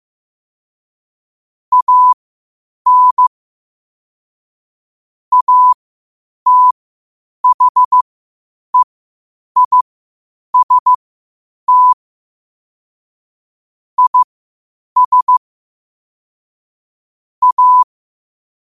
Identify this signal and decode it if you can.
Morse code, heard as a keyed tone, about 15 wpm, spaced as for 8 wpm, 1000 Hz. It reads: AN ATHEIST IS A
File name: qotdfiles / 8wpm